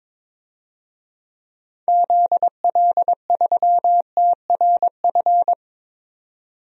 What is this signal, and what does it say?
Morse code, heard as a keyed tone, about 22 wpm, 700 Hz. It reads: ZL3TRF
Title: DXCC-100 Call Signs - 6 Characters - 22wpm